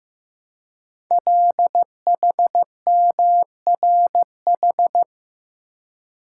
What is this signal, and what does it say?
Morse code, heard as a keyed tone, about 15 wpm, 700 Hz. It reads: LHMRH
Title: Code Groups - 5 Letters - 15wpm